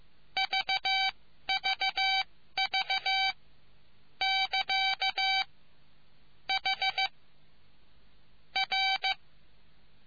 Morse code, heard as a keyed tone, about 15 wpm, spaced as for 12 wpm, 770 Hz. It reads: VVV <KA> H R